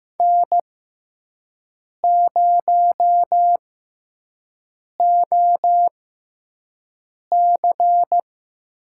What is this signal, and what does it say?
Morse code, heard as a keyed tone, about 15 wpm, spaced as for 5 wpm, 700 Hz. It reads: N0OC
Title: USA Call Signs - 5 Characters - 6x Character Spacing - 15wpm